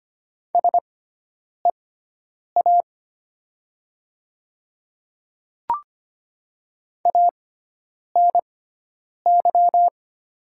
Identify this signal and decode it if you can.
Morse code, heard as a keyed tone, about 25 wpm, spaced as for 9 wpm, 700 Hz. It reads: SEA ANY
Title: 6x Character Spacing - Top 500 Words - 25wpm